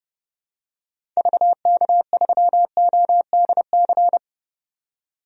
Morse code, heard as a keyed tone, about 30 wpm, 700 Hz. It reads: VK3ODC
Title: DXCC-100 Call Signs - 6 Characters - 30wpm